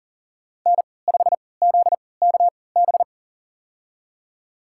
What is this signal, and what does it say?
Morse code, heard as a keyed tone, about 40 wpm, spaced as for 23 wpm, 700 Hz. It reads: N5ZKB
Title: USA Call Signs - 5 Characters - 3x Character Spacing - 40wpm